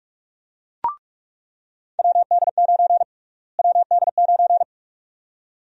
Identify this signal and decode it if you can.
Morse code, heard as a keyed tone, about 47 wpm, 700 Hz. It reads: WD9 WD9
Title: USA Call Sign Prefixes - Repeated 2x - 45wpm